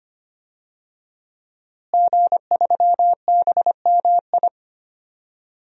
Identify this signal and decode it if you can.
Morse code, heard as a keyed tone, about 25 wpm, 700 Hz. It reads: G3BMI
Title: DXCC-100 Call Signs - 5 Characters - Repeated 2x - 25wpm